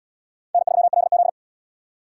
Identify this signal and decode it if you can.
Morse code, heard as a keyed tone, about 77 wpm, 700 Hz. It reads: N3LZ